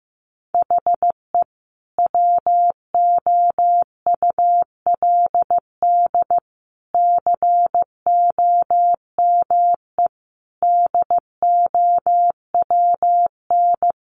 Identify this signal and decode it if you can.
Morse code, heard as a keyed tone, about 15 wpm, 700 Hz. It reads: HE WOULD COME DOWN
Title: Sentences from Top 1300 Words 15wpm